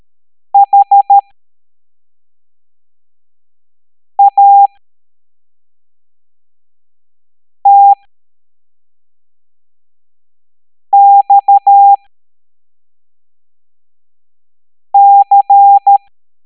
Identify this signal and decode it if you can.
Morse code, heard as a keyed tone, about 13 wpm, spaced as for 3 wpm, 800 Hz. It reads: HATXC